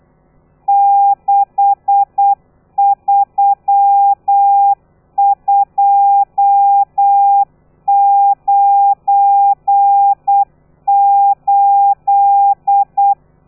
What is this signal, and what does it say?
Morse code, heard as a keyed tone, about 8 wpm, 800 Hz. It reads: 63298
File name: telegrafia / i2tfj / Lez12 35